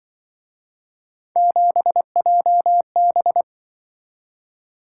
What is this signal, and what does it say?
Morse code, heard as a keyed tone, about 23 wpm, 700 Hz. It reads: 7JB
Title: Speed-Racing - Three Letter-Number Combinations - 20wpm